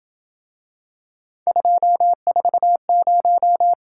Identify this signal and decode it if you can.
Morse code, heard as a keyed tone, about 26 wpm, 700 Hz. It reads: 240